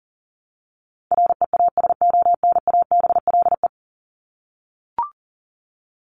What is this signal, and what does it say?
Morse code, heard as a keyed tone, about 40 wpm, 700 Hz. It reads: REASONABLE
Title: Top 1501-1750 Words 40wpm